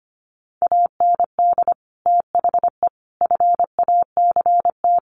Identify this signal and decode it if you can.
Morse code, heard as a keyed tone, about 25 wpm, 700 Hz. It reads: AND THE FACT
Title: Sets of 4 Words 25wpm